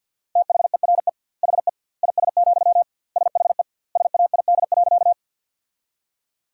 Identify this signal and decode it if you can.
Morse code, heard as a keyed tone, about 50 wpm, 700 Hz. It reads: THERE HE IS, SHE SAID.
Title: 2x Word-Spacing - Sentences from Top 100 Words - 50wpm